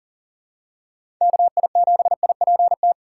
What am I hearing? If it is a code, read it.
Morse code, heard as a keyed tone, about 40 wpm, 700 Hz. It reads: KI7IPT
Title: USA Call Signs - 6 Characters - 2x Character Spacing - 40wpm